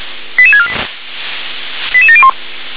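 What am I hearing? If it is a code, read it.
Five-tone selective call (ZVEI), heard as quick runs of five tones, tones of about 70 ms. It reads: 88648 88701
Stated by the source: ZVEI five-tone selective call 88770 received by SDR.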